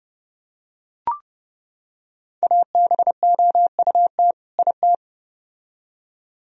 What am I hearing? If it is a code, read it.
Morse code, heard as a keyed tone, about 29 wpm, 700 Hz. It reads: ABOUT IT